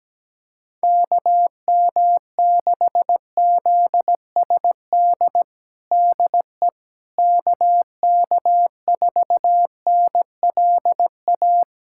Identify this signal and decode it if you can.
Morse code, heard as a keyed tone, about 17 wpm, 700 Hz. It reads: KM6ZSD DE KK4NLA
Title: USA Callsign DE Callsign - 6 Characters - 17wpm